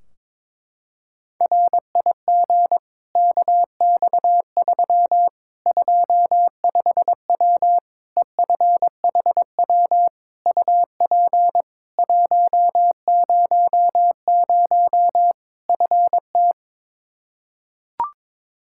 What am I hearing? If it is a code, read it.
Morse code, heard as a keyed tone, about 22 wpm, 700 Hz. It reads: RIG KX3 25W EFHW UP 100 FT
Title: Practice QSO - Exchange 2 - Info - 22wpm